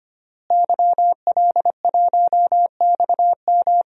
Morse code, heard as a keyed tone, about 25 wpm, 700 Hz. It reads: YL1XM
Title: DXCC-100 Call Signs - 5 Characters - Repeated 2x - 25wpm